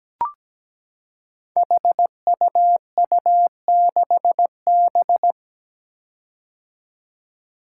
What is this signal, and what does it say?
Morse code, heard as a keyed tone, about 17 wpm, 700 Hz. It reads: HUU6B
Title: Code Groups - 5 Letter:Numbers - 17wpm